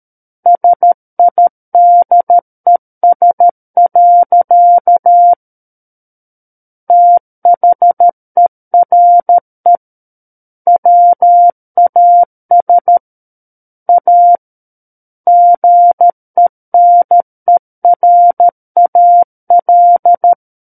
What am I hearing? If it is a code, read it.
Morse code, heard as a keyed tone, about 13 wpm, 700 Hz. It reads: SIDES. THERE WAS A GENERAL
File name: Book One / 04-Chapter 4 - 13 WPM